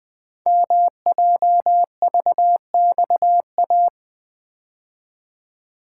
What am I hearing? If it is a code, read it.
Morse code, heard as a keyed tone, about 21 wpm, 700 Hz. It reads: MJVXA
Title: Code Groups - 5 Letters - 20wpm